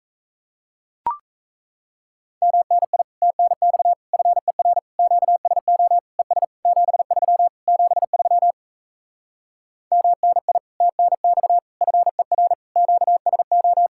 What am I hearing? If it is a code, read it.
Morse code, heard as a keyed tone, about 37 wpm, 700 Hz. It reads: MNI TNX FER QSO ES 73 73 MNI TNX FER QSO